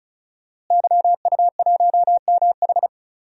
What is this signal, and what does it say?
Morse code, heard as a keyed tone, about 36 wpm, 700 Hz. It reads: YU1MH
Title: DXCC-100 Call Signs - 5 Characters - Repeated 3x - 35wpm